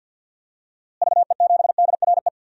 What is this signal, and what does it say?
Morse code, heard as a keyed tone, about 47 wpm, 700 Hz. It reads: VE7DRE
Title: DXCC-100 Call Signs - 6 Characters - Repeated 2x - 50wpm